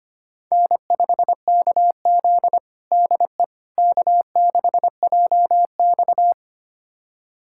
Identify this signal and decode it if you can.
Morse code, heard as a keyed tone, about 25 wpm, 700 Hz. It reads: N5KZ DE K6JX
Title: USA Callsign DE Callsign - 4 Characters - 25wpm